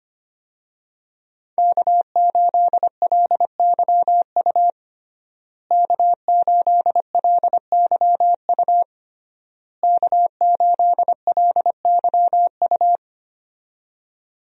Morse code, heard as a keyed tone, about 25 wpm, 700 Hz. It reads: K8LYU K8LYU K8LYU
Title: USA Call Signs - 5 Characters - Repeated 3x - 25wpm